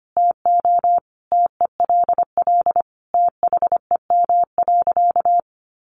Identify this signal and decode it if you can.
Morse code, heard as a keyed tone, about 25 wpm, 700 Hz. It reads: TO TELL THEM.
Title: Sentences from Top 700 Words 25wpm